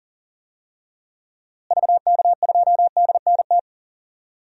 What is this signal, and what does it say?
Morse code, heard as a keyed tone, about 40 wpm, 700 Hz. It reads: VK2DNT
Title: DXCC-100 Call Signs - 6 Characters - Repeated 3x - 40wpm